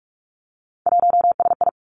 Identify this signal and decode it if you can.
Morse code, heard as a keyed tone, about 46 wpm, 700 Hz. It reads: 1SI
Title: Three Challenging Letter-Number Combinations 45wpm